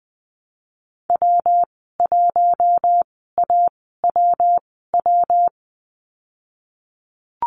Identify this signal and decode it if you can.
Morse code, heard as a keyed tone, about 20 wpm, spaced as for 15 wpm, 700 Hz. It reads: W1AWW
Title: 219 Lesson 35g - Number 0 - Call Signs 20wpm